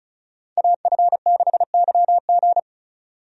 Morse code, heard as a keyed tone, about 36 wpm, 700 Hz. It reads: AF6YG
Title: USA Call Signs - 5 Characters - 35wpm